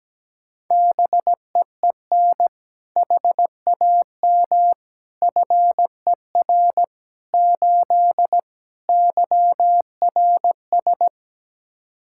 Morse code, heard as a keyed tone, about 17 wpm, 700 Hz. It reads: BEEN HAM FER 8 YRS
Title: Practice QSO - Exchange 3 - More Info - 17wpm